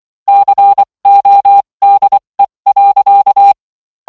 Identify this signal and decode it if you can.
Morse code, heard as a keyed tone, about 24 wpm, spaced as for 20 wpm, 800 Hz. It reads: CODE.